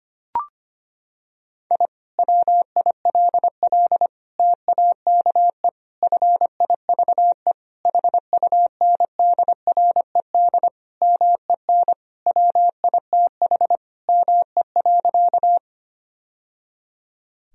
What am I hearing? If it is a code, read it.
Morse code, heard as a keyed tone, about 25 wpm, 700 Hz. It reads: I WILL TAKE FIVE HUNDRED MEN WITH ME.
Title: Sentences from Top 1000 Words - Review Part 2 - 25wpm